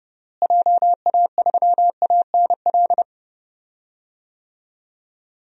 Morse code, heard as a keyed tone, about 30 wpm, 700 Hz. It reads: JA3ANL
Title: DXCC-100 Call Signs - 6 Characters - Repeated 2x - 30wpm